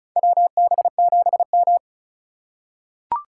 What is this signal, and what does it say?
Morse code, heard as a keyed tone, about 36 wpm, 700 Hz. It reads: WB7M
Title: USA Call Signs - 4 Characters - 8x Character Spacing - 35wpm